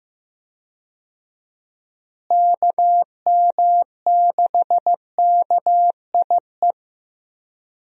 Morse code, heard as a keyed tone, about 14 wpm, 700 Hz. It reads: KM6KIE
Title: USA Call Signs - 6 Characters - Repeated 3x - 15wpm